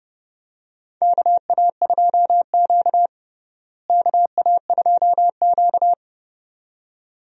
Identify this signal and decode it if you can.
Morse code, heard as a keyed tone, about 30 wpm, 700 Hz. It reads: KA2Q KA2Q
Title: USA Call Signs - 4 Characters - Repeated 2x - 30wpm